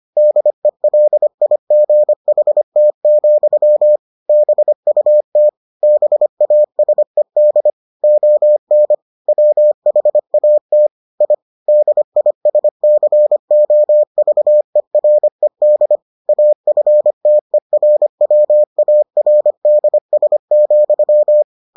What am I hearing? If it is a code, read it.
Morse code, heard as a keyed tone, about 25 wpm, 600 Hz. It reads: DELIGHT, BUT BASED ON WHAT I DISCOVERED AFTERWARDS,